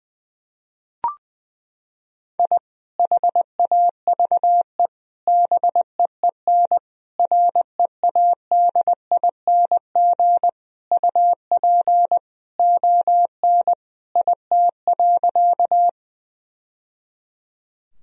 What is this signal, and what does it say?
Morse code, heard as a keyed tone, about 20 wpm, 700 Hz. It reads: I HAVE BEEN READING UP ON IT.
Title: Sentences from Top 1000 Words - Review Part 3 - 20wpm